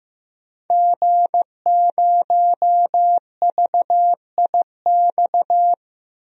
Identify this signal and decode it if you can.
Morse code, heard as a keyed tone, about 15 wpm, 700 Hz. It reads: G0VIX
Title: DXCC-100 Call Signs - 5 Characters - Repeated 3x - 15wpm